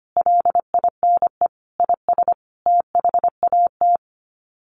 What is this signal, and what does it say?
Morse code, heard as a keyed tone, about 25 wpm, 700 Hz. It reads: LINE IS THAT
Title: Sets of 5 Words Encore 25wpm